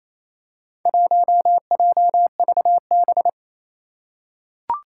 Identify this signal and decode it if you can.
Morse code, heard as a keyed tone, about 29 wpm, 700 Hz. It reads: 1JVB
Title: Four Challenging Letter-Number Combinations 28wpm